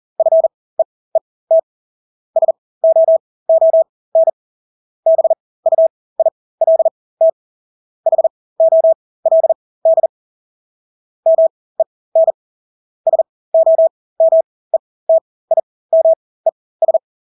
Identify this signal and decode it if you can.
Morse code, heard as a keyed tone, about 40 wpm, spaced as for 20 wpm, 650 Hz. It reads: FEET SOON BUILT HOLD MEN SOMETIMES
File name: Engelske ord 40-20 wpm